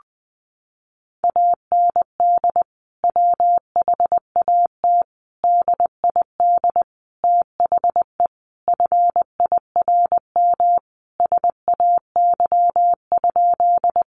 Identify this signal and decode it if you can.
Morse code, heard as a keyed tone, about 20 wpm, 700 Hz. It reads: AND WHAT DID THE FIRM SAY?
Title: Sentences from Top 1400 Words 20wpm